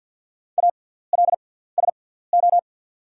Morse code, heard as a keyed tone, about 48 wpm, spaced as for 17 wpm, 690 Hz. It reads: ALSO
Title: 6x Character Spacing - Top 200 Words - 50wpm